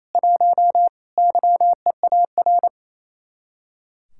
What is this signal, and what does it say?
Morse code, heard as a keyed tone, about 35 wpm, 700 Hz. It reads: 1 YEAR
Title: Speed-Racing Common QSO Phrases Encore - 28wpm